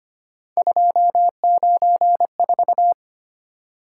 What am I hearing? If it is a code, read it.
Morse code, heard as a keyed tone, about 25 wpm, 700 Hz. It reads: 294
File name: Three Letter-Number Combinations - 25wpm